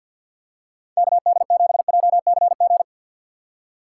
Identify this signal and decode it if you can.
Morse code, heard as a keyed tone, about 49 wpm, 700 Hz. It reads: KD7JCG
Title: USA Call Signs - 6 Characters - 3x Character Spacing - 50wpm